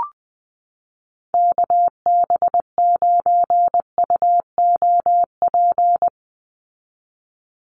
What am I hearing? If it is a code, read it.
Morse code, heard as a keyed tone, about 20 wpm, 700 Hz. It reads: KB9UOP
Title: USA Call Signs 20wpm